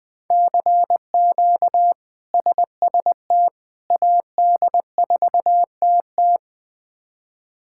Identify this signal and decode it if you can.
Morse code, heard as a keyed tone, about 20 wpm, 700 Hz. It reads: CQ SST AD4TT
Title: K1USN SST QSO Practice - 20wpm